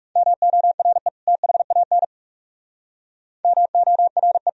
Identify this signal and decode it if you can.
Morse code, heard as a keyed tone, about 38 wpm, 700 Hz. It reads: MORE THAN MORE